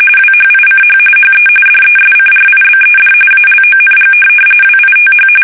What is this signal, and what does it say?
RTTY, about 45.45 baud. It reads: WHICH HE DESCRIBED ALONG WITH
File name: fsk850 / benjamin-g-lamme